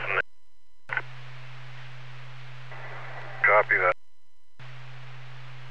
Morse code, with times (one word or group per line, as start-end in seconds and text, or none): none